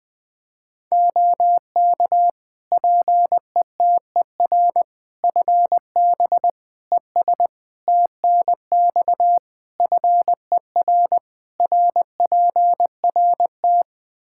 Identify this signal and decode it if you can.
Morse code, heard as a keyed tone, about 20 wpm, 700 Hz. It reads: OK PETER FB ES TNX FER RPRT
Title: K1USN SST QSO Practice 20wpm